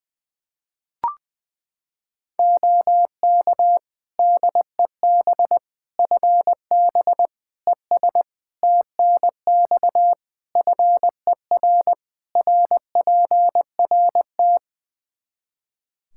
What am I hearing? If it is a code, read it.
Morse code, heard as a keyed tone, about 20 wpm, 700 Hz. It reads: OK DEB FB ES TNX FER RPRT
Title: Practice QSO - Exchange 2 - Info - 20wpm